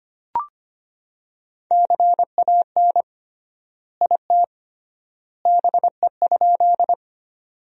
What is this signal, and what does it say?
Morse code, heard as a keyed tone, about 25 wpm, 700 Hz. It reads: CAN IT BE?
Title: 3x Word-Spacing - Sentences from Top 100 Words - 25wpm